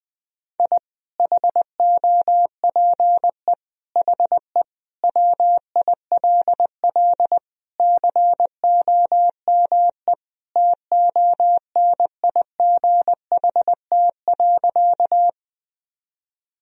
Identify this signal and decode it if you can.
Morse code, heard as a keyed tone, about 21 wpm, 700 Hz. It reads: I HOPE HE WILL COME TONIGHT.
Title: Sentences from Top 1000 Words - Review Part 3 - 20wpm